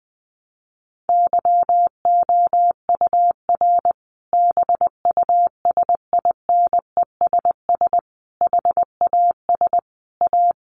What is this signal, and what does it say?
Morse code, heard as a keyed tone, about 20 wpm, 700 Hz. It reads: YOUR BUSINESS HAS A
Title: Sentences from Top 1200 Words 20wpm